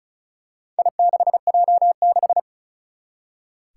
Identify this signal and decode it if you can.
Morse code, heard as a keyed tone, about 36 wpm, 700 Hz. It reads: I6J6
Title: Four Challenging Letter-Number Combinations 35wpm